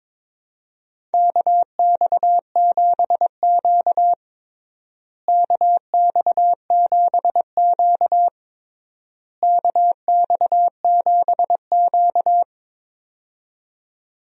Morse code, heard as a keyed tone, about 22 wpm, 700 Hz. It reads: KX7Q KX7Q KX7Q
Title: USA Call Signs - 4 Characters - Repeated 3x - 22wpm